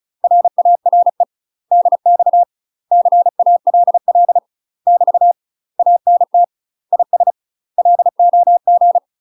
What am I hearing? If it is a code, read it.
Morse code, heard as a keyed tone, about 34 wpm, 700 Hz. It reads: RARE DX CALL <BT> ANT IS LOG
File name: UK MASTER QSOs TRAINING / 35WPM / UK MASTER QSOs 0019